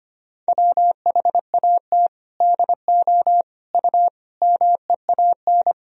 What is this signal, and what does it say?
Morse code, heard as a keyed tone, about 25 wpm, 700 Hz. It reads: WHAT DO U MEAN
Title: Sentences from Top 400 Words and Abbreviations 25wpm